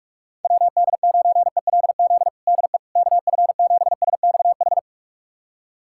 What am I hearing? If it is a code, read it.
Morse code, heard as a keyed tone, about 47 wpm, 700 Hz. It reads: WD9ELZ DE KF7SXH
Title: USA Callsign DE Callsign - 6 Characters - 45wpm